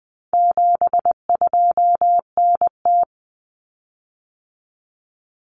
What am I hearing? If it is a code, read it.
Morse code, heard as a keyed tone, about 20 wpm, 700 Hz. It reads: 72NT
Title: Common Cut Numbers - 4-Digit 20wpm